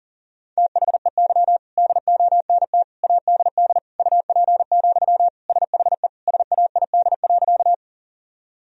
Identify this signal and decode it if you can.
Morse code, heard as a keyed tone, about 40 wpm, 700 Hz. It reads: THEY DONT ADD UP, SHE SAID.
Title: Sentences from Top 1000 Words - Review Part 1 - 40wpm